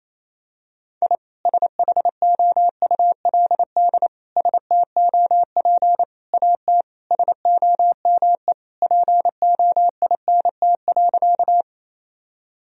Morse code, heard as a keyed tone, about 28 wpm, 700 Hz. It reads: I SHOULD STOP AT SOME POINT.
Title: Sentences from Top 1000 Words - Review Part 1 - 28wpm